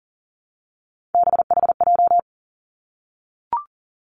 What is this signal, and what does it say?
Morse code, heard as a keyed tone, about 42 wpm, 700 Hz. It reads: BHJ
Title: Three Challenging Letter-Number Combinations 40wpm